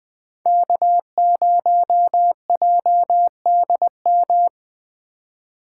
Morse code, heard as a keyed tone, about 20 wpm, 700 Hz. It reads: K0JDM